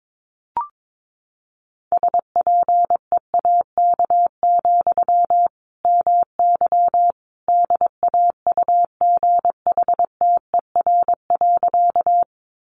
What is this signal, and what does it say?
Morse code, heard as a keyed tone, about 22 wpm, 700 Hz. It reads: SPEAK, MY DAUGHTER.
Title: Sentences from Top 1500 Words 22wpm